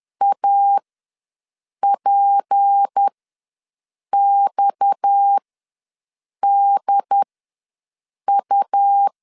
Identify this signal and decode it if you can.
Morse code, heard as a keyed tone, about 10 wpm, spaced as for 6 wpm, 800 Hz. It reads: APXDU
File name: LEZIONI / 6 PKX0 / 6 PKX ws 10wpm 3706